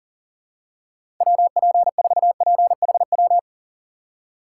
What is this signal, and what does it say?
Morse code, heard as a keyed tone, about 40 wpm, 700 Hz. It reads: WP4PHW
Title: USA Call Signs - 6 Characters - 4x Character Spacing - 40wpm